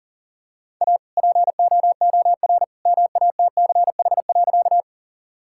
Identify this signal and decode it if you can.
Morse code, heard as a keyed tone, about 37 wpm, 700 Hz. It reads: A POOR MATCH.
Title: Sentences from Top 1600 Words 40wpm